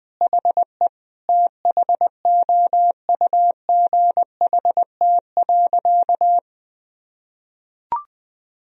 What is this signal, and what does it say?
Morse code, heard as a keyed tone, about 20 wpm, 700 Hz. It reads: HE THOUGHT.
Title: Sentences from Top 1000 Words - Review Part 3 - 20wpm